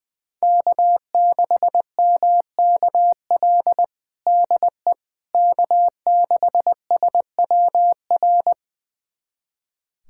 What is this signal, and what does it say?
Morse code, heard as a keyed tone, about 21 wpm, 700 Hz. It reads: K6MKL DE K6SWR
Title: USA Callsign DE Callsign - 5 Characters - 20wpm